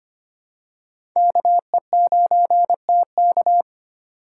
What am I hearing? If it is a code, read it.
Morse code, heard as a keyed tone, about 25 wpm, 700 Hz. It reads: KE9TK
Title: USA Call Signs - 5 Characters - 4x Character Spacing - 25wpm